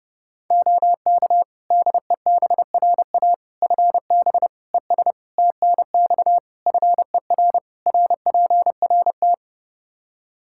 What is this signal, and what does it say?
Morse code, heard as a keyed tone, about 30 wpm, 700 Hz. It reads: OK DEBRA FB ES TNX FER RPRT
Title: Practice QSO - Exchange 2 - Info - 30wpm